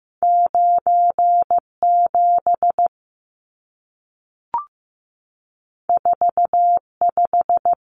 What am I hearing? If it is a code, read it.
Morse code, heard as a keyed tone, about 15 wpm, 700 Hz. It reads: 97 45
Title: Numbers - 2-Digit 15wpm